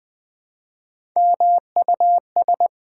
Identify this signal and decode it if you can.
Morse code, heard as a keyed tone, about 20 wpm, 690 Hz. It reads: MUS
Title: Build A Word - 3 Letters At A Time - 8-Letter Words - 20wpm